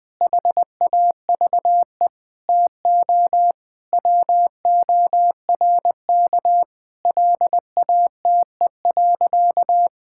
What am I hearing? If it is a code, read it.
Morse code, heard as a keyed tone, about 20 wpm, 700 Hz. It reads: HAVE TO WORK LATE.
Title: Sentences with 5-Word Phrases - Spoken Beginning - 20wpm